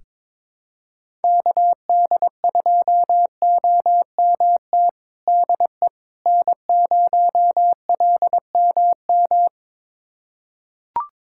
Morse code, heard as a keyed tone, about 22 wpm, 700 Hz. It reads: KD2OMT DE N0LMM
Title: USA Callsign DE Callsign 22wpm